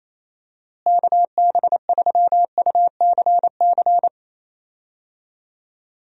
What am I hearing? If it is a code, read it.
Morse code, heard as a keyed tone, about 27 wpm, 700 Hz. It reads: KB3UCC
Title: USA Call Signs - 6 Characters - Repeated 3x - 28wpm